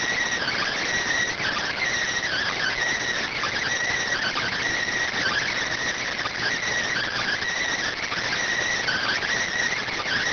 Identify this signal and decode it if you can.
Emergency Alert System (SAME) tones (header bursts), heard as harsh buzzing bursts. It is heard from 0.4 to 9.2 s.